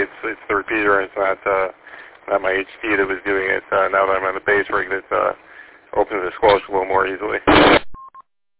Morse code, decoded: none

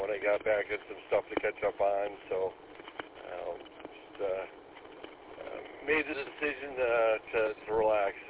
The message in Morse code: none

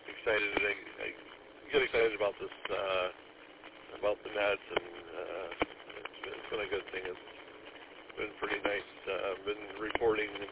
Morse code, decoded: none

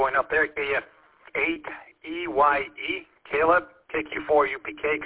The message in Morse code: none

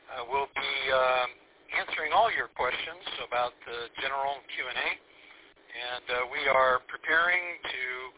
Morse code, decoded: none

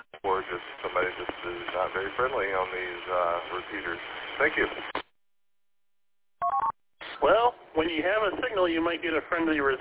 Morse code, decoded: none

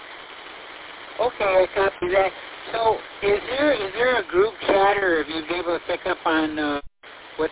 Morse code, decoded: none

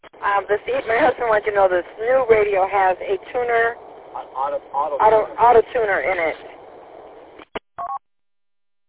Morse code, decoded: none

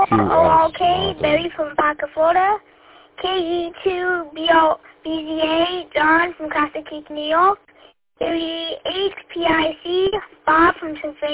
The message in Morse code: none

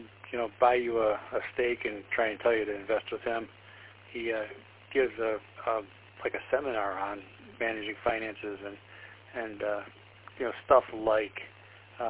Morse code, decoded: none

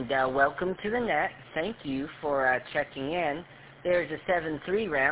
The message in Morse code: none

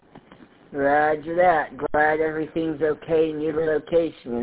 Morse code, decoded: none

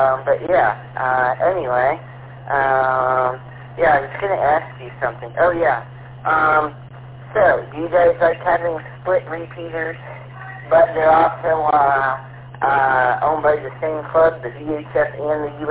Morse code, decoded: none

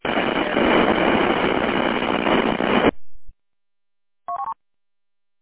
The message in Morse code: none

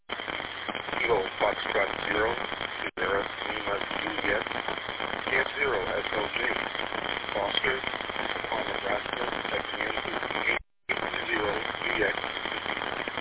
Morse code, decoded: none

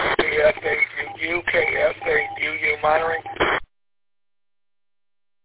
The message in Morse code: none

